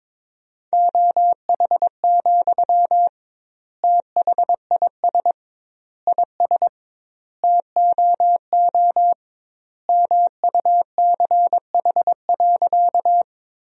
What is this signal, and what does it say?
Morse code, heard as a keyed tone, about 22 wpm, 700 Hz. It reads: OH, THIS IS TOO MUCH.
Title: 2x Word-Spacing - Sentences from Top 500 Words - 22wpm